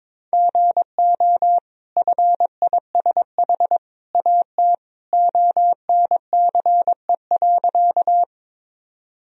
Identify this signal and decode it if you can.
Morse code, heard as a keyed tone, about 22 wpm, 700 Hz. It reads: GO FISH AT ONCE.
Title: Sentences from Top 1000 Words - Review Part 1 - 22wpm